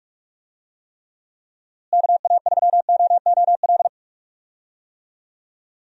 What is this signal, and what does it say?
Morse code, heard as a keyed tone, about 41 wpm, 700 Hz. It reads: KA3OOL
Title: USA Call Signs - 6 Characters - Repeated 3x - 45wpm